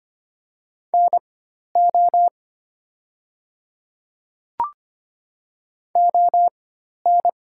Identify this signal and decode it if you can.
Morse code, heard as a keyed tone, about 25 wpm, spaced as for 12 wpm, 700 Hz. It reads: NO ON
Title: 4x Character Spacing - Top 100 Words - 25wpm